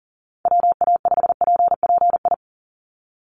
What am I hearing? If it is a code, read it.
Morse code, heard as a keyed tone, about 40 wpm, 700 Hz. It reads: WA5PPI
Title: USA Call Signs 40wpm